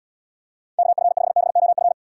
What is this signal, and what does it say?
Morse code, heard as a keyed tone, about 78 wpm, 700 Hz. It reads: KF5BOF